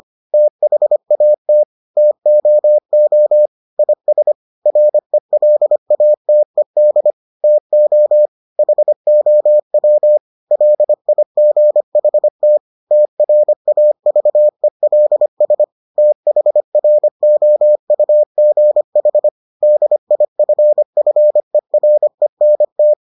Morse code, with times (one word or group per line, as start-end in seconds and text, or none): 0.3-1.6 THAT
2.0-3.5 TOO
3.8-4.3 IS
4.7-7.1 RELATED
7.4-8.3 TO
8.6-10.2 HOW
10.5-12.6 LIGHT
12.9-15.6 TRAVELS
16.0-19.3 THROUGH
19.6-22.9 DIFFERENT